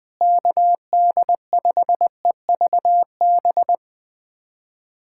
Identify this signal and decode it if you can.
Morse code, heard as a keyed tone, about 21 wpm, 700 Hz. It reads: KD5EVB
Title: USA Callsign DE Callsign - 6 Characters - 20wpm